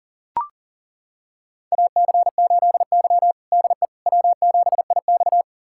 Morse code, heard as a keyed tone, about 40 wpm, 700 Hz. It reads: AC8Y DE W7IX
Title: USA Callsign DE Callsign - 4 Characters - 40wpm